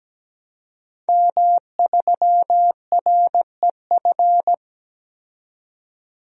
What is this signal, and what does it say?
Morse code, heard as a keyed tone, about 17 wpm, 700 Hz. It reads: M3REF
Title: DXCC-100 Call Signs - 5 Characters - 17wpm